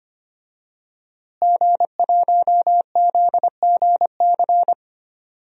Morse code, heard as a keyed tone, about 25 wpm, 700 Hz. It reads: G1ZGC
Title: DXCC-100 Call Signs - 5 Characters - Repeated 3x - 25wpm